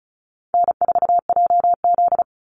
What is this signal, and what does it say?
Morse code, heard as a keyed tone, about 34 wpm, 700 Hz. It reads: N4JZ
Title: USA Call Signs 35wpm